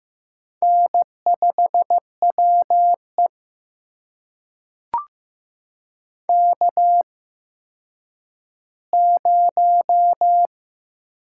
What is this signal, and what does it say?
Morse code, heard as a keyed tone, about 15 wpm, spaced as for 9 wpm, 700 Hz. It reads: N5WE K0
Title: USA Call Signs - 4 Characters - 8x Character Spacing - 15wpm